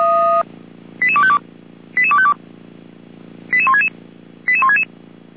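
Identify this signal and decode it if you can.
Five-tone selective call (ZVEI), heard as quick runs of five tones, tones of about 70 ms. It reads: 88252 88252 88166 88166
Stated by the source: ZVEI five-tone selective call received on an SDR.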